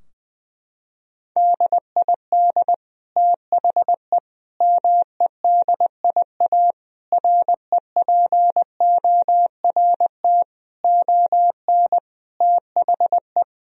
Morse code, heard as a keyed tone, about 20 wpm, 700 Hz. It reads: DID THE MEDIA REPORT ON THE
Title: Sentences from Top 1000 Words - Review Part 4 - 20wpm